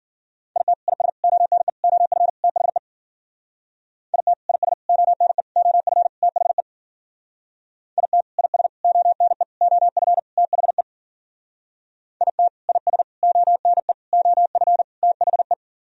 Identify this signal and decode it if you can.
Morse code, heard as a keyed tone, about 49 wpm, 700 Hz. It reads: IT IS ONE OF THE IT IS ONE OF THE IT IS ONE OF THE IT IS ONE OF THE
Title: Speed-Racing - Sets of 5 Words 40wpm